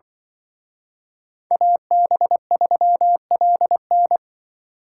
Morse code, heard as a keyed tone, about 24 wpm, 700 Hz. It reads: AB3LN
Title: Speed Racing - USA 2x2 Call Signs 20wpm